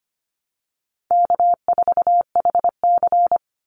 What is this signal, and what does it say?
Morse code, heard as a keyed tone, about 25 wpm, 700 Hz. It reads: K4HC
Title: Easy Call Signs Repeated 2x - 25wpm